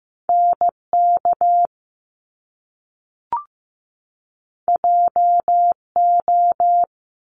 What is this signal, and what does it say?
Morse code, heard as a keyed tone, about 15 wpm, 700 Hz. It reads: NK JO
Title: Top Two Letter Combinations in Words 15wpm